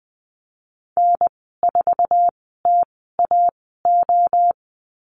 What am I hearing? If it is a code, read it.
Morse code, heard as a keyed tone, about 20 wpm, spaced as for 14 wpm, 700 Hz. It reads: N4TAO
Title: 055 Lesson 10g - Letter r - Call Signs 20wpm